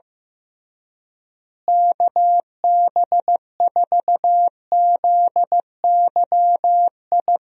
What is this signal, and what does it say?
Morse code, heard as a keyed tone, about 16 wpm, 700 Hz. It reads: KB4ZYI
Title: USA Call Signs - 6 Characters - Repeated 2x - 15wpm